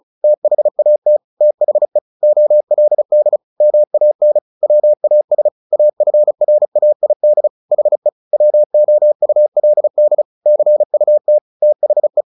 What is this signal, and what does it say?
Morse code, heard as a keyed tone, about 34 wpm, 600 Hz. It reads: THAT THE OLD MAN WAS AFRAID HE WOULD CUT THE
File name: the old man and the sea / the old man and the sea 1 35wpm